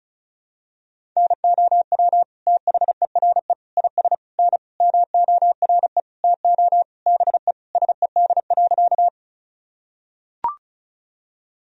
Morse code, heard as a keyed tone, about 34 wpm, 700 Hz. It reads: NOW THERE IS N MORE TO BE SED.
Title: Sentences from Top 100 Words and Abbreviations 35wpm